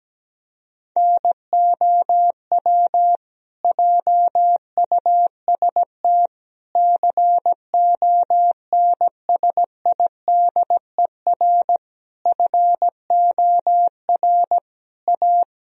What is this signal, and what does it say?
Morse code, heard as a keyed tone, about 17 wpm, 700 Hz. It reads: NOW JUST CONSIDER FOR A
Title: Sentences from Top 1000 Words - Review Part 4 - 17wpm